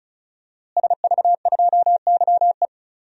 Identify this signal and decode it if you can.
Morse code, heard as a keyed tone, about 36 wpm, 700 Hz. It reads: SV2YE
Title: DXCC-100 Call Signs - 5 Characters - Repeated 3x - 35wpm